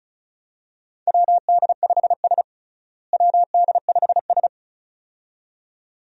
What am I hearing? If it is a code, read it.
Morse code, heard as a keyed tone, about 34 wpm, 700 Hz. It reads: WD5S WD5S
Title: USA Call Signs - 4 Characters - Repeated 2x - 35wpm